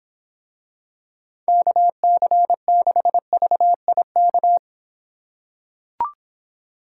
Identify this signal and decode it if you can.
Morse code, heard as a keyed tone, about 26 wpm, 700 Hz. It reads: KC6VIK